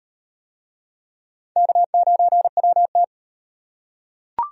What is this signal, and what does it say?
Morse code, heard as a keyed tone, about 38 wpm, 700 Hz. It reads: K9WT